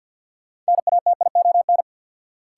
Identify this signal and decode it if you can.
Morse code, heard as a keyed tone, about 52 wpm, 700 Hz. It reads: NATION
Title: Sets of 3 Words - New Word Drill - 50wpm